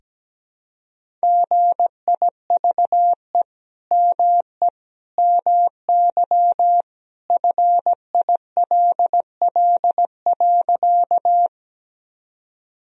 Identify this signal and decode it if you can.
Morse code, heard as a keyed tone, about 17 wpm, 700 Hz. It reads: GIVE ME MY FILL.
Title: Sentences from Top 1000 Words - Review Part 3 - 17wpm